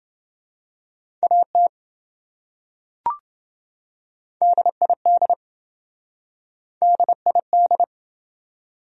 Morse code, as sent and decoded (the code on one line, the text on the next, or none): .- - / -.. .. -.. / -.. .. -..
AT DID DID